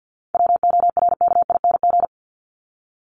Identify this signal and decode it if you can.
Morse code, heard as a keyed tone, about 50 wpm, 700 Hz. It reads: WORKING
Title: ING Suffix - Top 100 Words with Base Words 50wpm